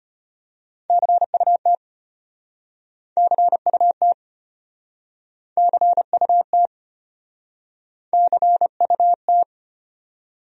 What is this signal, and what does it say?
Morse code, heard as a keyed tone, about 35 wpm, 700 Hz. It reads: CUT CUT CUT CUT